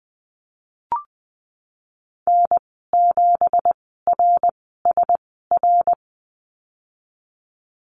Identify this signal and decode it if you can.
Morse code, heard as a keyed tone, about 20 wpm, spaced as for 15 wpm, 700 Hz. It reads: N7RSR
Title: 167 Lesson 27g - Number 7 - Call Signs 20wpm